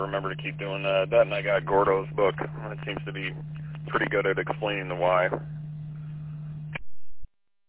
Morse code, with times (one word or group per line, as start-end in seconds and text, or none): none